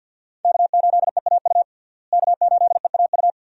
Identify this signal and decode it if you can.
Morse code, heard as a keyed tone, about 52 wpm, 700 Hz. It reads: K8EAU K8EAU
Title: USA Call Signs - 5 Characters - Repeated 2x - 50wpm